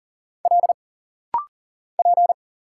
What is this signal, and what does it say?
Morse code, heard as a keyed tone, about 40 wpm, 700 Hz. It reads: L P
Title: Single Letter-Number Alt - Rapid-Fire 40wpm